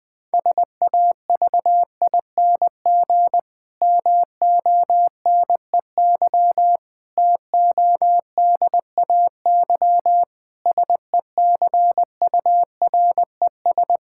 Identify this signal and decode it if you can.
Morse code, heard as a keyed tone, about 20 wpm, 700 Hz. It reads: SAVING MONEY TODAY SECURES
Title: Difficult Sentences with Hints - 20wpm